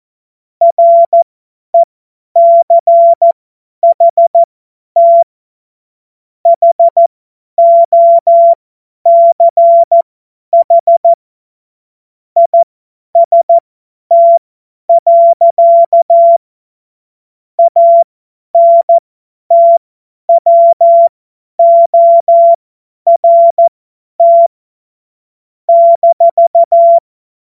Morse code, heard as a keyed tone, about 14 wpm, spaced as for 10 wpm, 680 Hz. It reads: RECHT HOCH IST. ANTWORT -
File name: Morsekurs / Witze / Witz 05. April 2022 50 F70